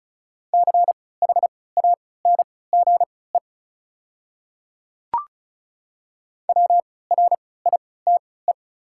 Morse code, heard as a keyed tone, about 36 wpm, spaced as for 20 wpm, 700 Hz. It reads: CHANGE WRITE
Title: 3x Character Spacing - Top 200 Words - 35wpm